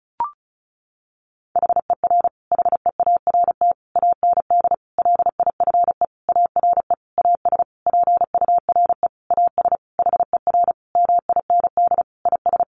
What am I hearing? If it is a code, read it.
Morse code, heard as a keyed tone, about 34 wpm, 700 Hz. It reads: HER HEART AND LIFE ARE AS PURE AS HER MIND IS
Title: Sentences from Top 1500 Words 35wpm